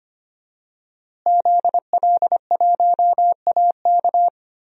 Morse code, heard as a keyed tone, about 25 wpm, 700 Hz. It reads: ZL1AK
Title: DXCC-100 Call Signs - 5 Characters - Repeated 3x - 25wpm